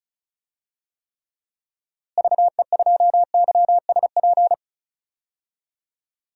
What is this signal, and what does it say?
Morse code, heard as a keyed tone, about 36 wpm, 700 Hz. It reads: VE2YSP
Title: DXCC-100 Call Signs - 6 Characters - Repeated 2x - 35wpm